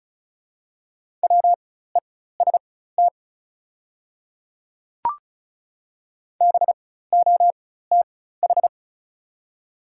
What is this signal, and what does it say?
Morse code, heard as a keyed tone, about 36 wpm, spaced as for 16 wpm, 700 Hz. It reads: WEST BOTH
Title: 4x Character Spacing - Top 500 Words - 35wpm